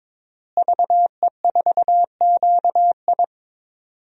Morse code, heard as a keyed tone, about 22 wpm, 700 Hz. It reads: VE4QI
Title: DXCC-100 Call Signs - 5 Characters - Repeated 3x - 22wpm